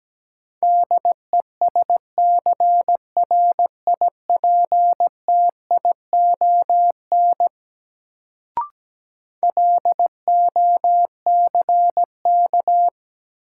Morse code, heard as a keyed tone, about 17 wpm, 700 Hz. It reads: DESCRIPTION LOCK
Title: Sentences from Top 1300 Words - New Word Drill Alt - 17wpm